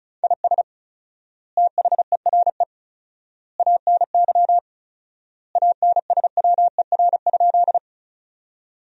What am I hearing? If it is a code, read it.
Morse code, heard as a keyed tone, about 34 wpm, 700 Hz. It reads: IS THERE ANY ANSWER?
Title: 4x Word-Spacing - Sentences from Top 300 Words - 35wpm